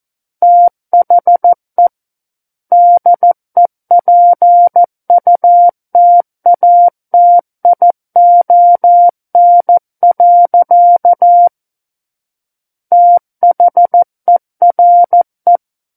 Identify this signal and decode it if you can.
Morse code, heard as a keyed tone, about 14 wpm, 700 Hz. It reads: THE DEPUTATION. THERE
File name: Book One / 05-Chapter 5 - 14 WPM